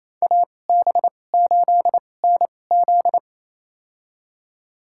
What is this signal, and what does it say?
Morse code, heard as a keyed tone, about 28 wpm, spaced as for 20 wpm, 700 Hz. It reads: AB8NZ
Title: USA Call Signs - 5 Characters - 2x Character Spacing - 28wpm